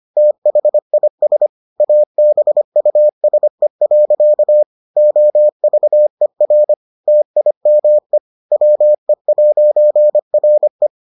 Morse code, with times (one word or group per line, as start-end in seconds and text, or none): 0.2-1.5 THIS
1.8-4.6 ABUSE.
5.0-6.7 OVER
7.1-8.2 TIME
8.5-10.9 WE'RE